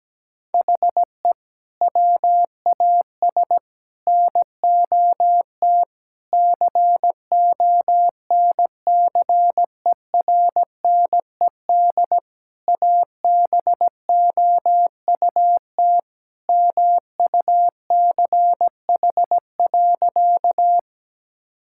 Morse code, heard as a keyed tone, about 17 wpm, 700 Hz. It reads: HE WAS NOT CONCERNED ABOUT MUCH.
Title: Sentences from Top 1000 Words - Review Part 6 - 17wpm